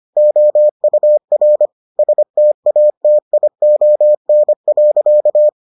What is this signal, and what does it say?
Morse code, heard as a keyed tone, about 25 wpm, 600 Hz. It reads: OUR STATION.